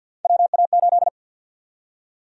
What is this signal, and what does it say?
Morse code, heard as a keyed tone, about 51 wpm, 700 Hz. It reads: WA8
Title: USA Call Sign Prefixes - Repeated 2x - 50wpm